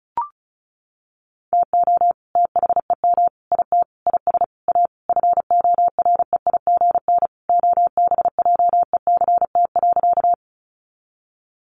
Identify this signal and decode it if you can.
Morse code, heard as a keyed tone, about 34 wpm, 700 Hz. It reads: TO THEM IT IS A FOREIGN OBJECT.
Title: Sentences from Top 1400 Words 35wpm